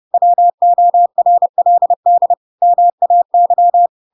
Morse code, heard as a keyed tone, about 30 wpm, 700 Hz. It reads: WORLD MAY
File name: Pinocchio / Pinocchio ch7 30 30 ik1hge